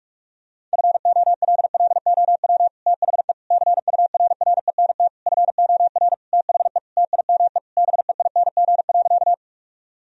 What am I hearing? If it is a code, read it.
Morse code, heard as a keyed tone, about 46 wpm, 700 Hz. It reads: FOLLOW THE CURRENT FOR THE TIME BEING.
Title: Sentences from Top 1000 Words - Review Part 4 - 45wpm